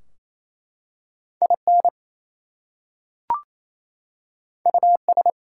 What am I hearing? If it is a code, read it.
Morse code, heard as a keyed tone, about 27 wpm, 700 Hz. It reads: IN US
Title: Two Letter Words - 28wpm